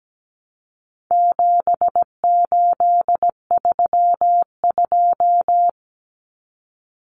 Morse code, heard as a keyed tone, about 17 wpm, 700 Hz. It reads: 7832
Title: Common Cut Numbers - 4-Digit 17wpm